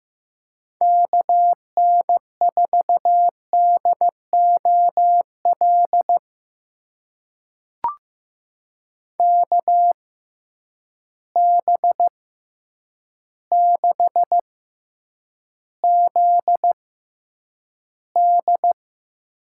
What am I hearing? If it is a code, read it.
Morse code, heard as a keyed tone, about 15 wpm, spaced as for 8 wpm, 700 Hz. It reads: KN4DOL KB6ZD
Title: USA Call Signs - 6x Character Spacing - 15wpm